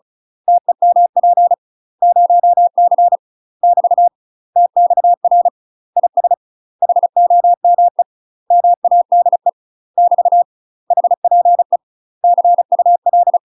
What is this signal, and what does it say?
Morse code, heard as a keyed tone, about 34 wpm, 700 Hz. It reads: TEMP 0C <BT> TXR IS HOME MADE <BT> HPE CUL
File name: UK MASTER QSOs TRAINING / 35WPM / UK MASTER QSOs 0004